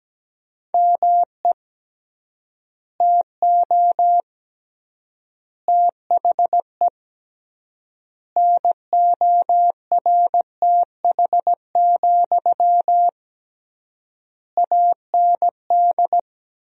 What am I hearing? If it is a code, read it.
Morse code, heard as a keyed tone, about 17 wpm, 700 Hz. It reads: ME TO THE NORTH, AND
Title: 3x Word-Spacing - Sentences from Top 400 Words - 17wpm